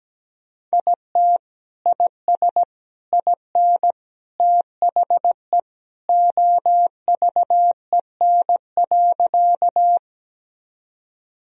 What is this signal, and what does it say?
Morse code, heard as a keyed tone, about 17 wpm, 700 Hz. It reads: IT IS IN THE OVEN.
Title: Sentences from Top 1000 Words - Review Part 4 - 17wpm